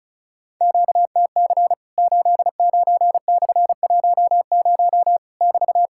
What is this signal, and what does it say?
Morse code, heard as a keyed tone, about 36 wpm, 700 Hz. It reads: QTC 89/10 <BT>
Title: Worked All Europe DX Contest - QTC Exchange - 35wpm